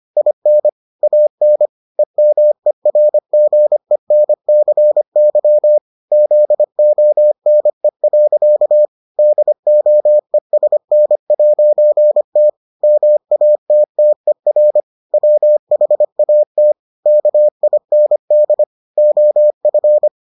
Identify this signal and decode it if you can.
Morse code, heard as a keyed tone, about 25 wpm, 600 Hz. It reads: IN AN EMERGENCY ZONE. DOESN'T MATTER WHAT KIND OF